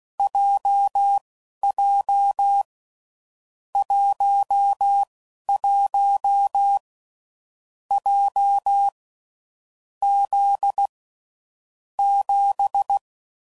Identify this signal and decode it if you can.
Morse code, heard as a keyed tone, about 16 wpm, spaced as for 12 wpm, 800 Hz. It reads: JJ 11 J Z 7
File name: Groups / Basic Letter Groups / JZ17